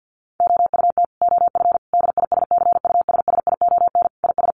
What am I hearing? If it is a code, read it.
Morse code, heard as a keyed tone, about 50 wpm, 700 Hz. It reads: OUT OF DISCUSSION IS
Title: Sentences from Top 1100 Words 50wpm